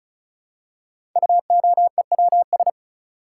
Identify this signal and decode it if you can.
Morse code, heard as a keyed tone, about 36 wpm, 700 Hz. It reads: UOEWS
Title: Code Groups - 5 Letters - 35wpm